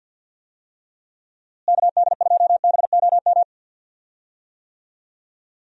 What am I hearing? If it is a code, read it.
Morse code, heard as a keyed tone, about 46 wpm, 700 Hz. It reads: KD2BOM
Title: USA Call Signs - 6 Characters - Repeated 3x - 50wpm